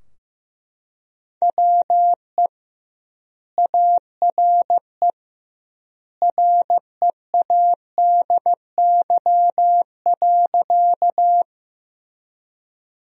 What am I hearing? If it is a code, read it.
Morse code, heard as a keyed tone, about 15 wpm, 700 Hz. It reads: WE ARE READY.
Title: 2x Word-Spacing - Sentences from Top 400 Words - 15wpm